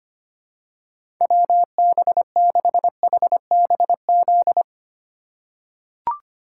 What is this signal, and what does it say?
Morse code, heard as a keyed tone, about 25 wpm, 700 Hz. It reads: WB6HBZ
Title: USA Call Signs - 6 Characters - 2x Character Spacing - 25wpm